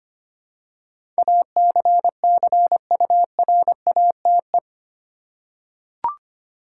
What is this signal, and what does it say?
Morse code, heard as a keyed tone, about 26 wpm, 700 Hz. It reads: ACCURATE
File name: 8 Letter Words - First 3 Letters Given - 25wpm